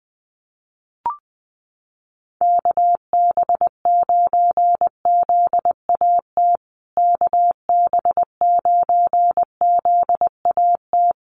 Morse code, heard as a keyed tone, about 20 wpm, 700 Hz. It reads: KB9ZAT KB9ZAT
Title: USA Call Signs Repeated 3x 20wpm